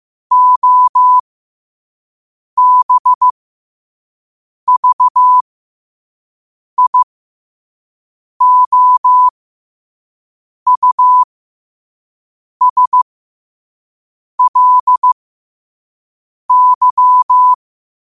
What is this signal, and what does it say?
Morse code, heard as a keyed tone, about 15 wpm, spaced as for 5 wpm, 1000 Hz. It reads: OBVIOUSLY
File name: qotdfiles / 5wpm